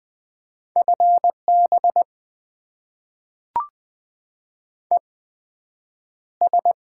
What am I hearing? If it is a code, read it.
Morse code, heard as a keyed tone, about 20 wpm, spaced as for 9 wpm, 700 Hz. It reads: FB ES
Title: Top 20 QSO Elements - 8x Character Spacing - 20wpm